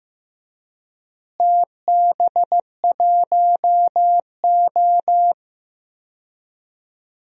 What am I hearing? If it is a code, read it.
Morse code, heard as a keyed tone, about 15 wpm, 700 Hz. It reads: TB1O
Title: DXCC-100 Call Signs - 4 Characters - Repeated 3x - 15wpm